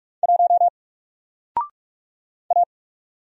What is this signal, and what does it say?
Morse code, heard as a keyed tone, about 44 wpm, 700 Hz. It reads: 1 A
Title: Single Letter-Number 45wpm